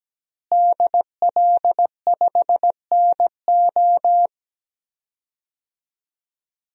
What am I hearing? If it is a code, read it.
Morse code, heard as a keyed tone, about 17 wpm, 700 Hz. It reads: DL5NO
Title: DXCC-100 Call Signs - 5 Characters - Repeated 3x - 17wpm